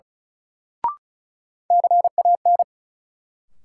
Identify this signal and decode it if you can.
Morse code, heard as a keyed tone, about 36 wpm, 700 Hz. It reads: CAN
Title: Sets of 2 Words - New Word Drill - 35wpm